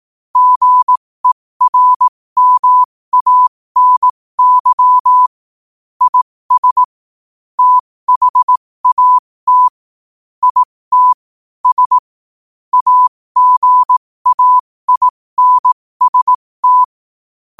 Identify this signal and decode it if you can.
Morse code, heard as a keyed tone, about 18 wpm, spaced as for 15 wpm, 1000 Hz. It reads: GERMANY IS THAT IT S AGAINST
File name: qotdfiles / 15wpm